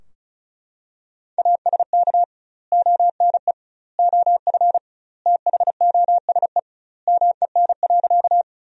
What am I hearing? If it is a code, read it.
Morse code, heard as a keyed tone, about 34 wpm, 700 Hz. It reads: ASK ONE OF THOSE MEN.
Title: 2x Word-Spacing - Sentences from Top 300 Words - 35wpm